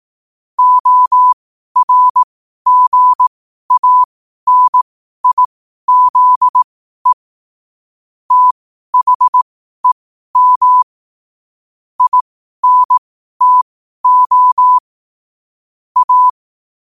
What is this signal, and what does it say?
Morse code, heard as a keyed tone, about 18 wpm, spaced as for 12 wpm, 1000 Hz. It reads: ORGANIZE THEM INTO A
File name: qotdfiles / 12wpm